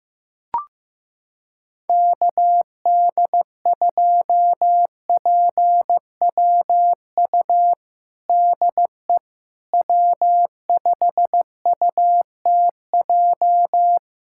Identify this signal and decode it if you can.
Morse code, heard as a keyed tone, about 15 wpm, 700 Hz. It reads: KD2PWU DE W5UTJ
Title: USA Callsign DE Callsign 15wpm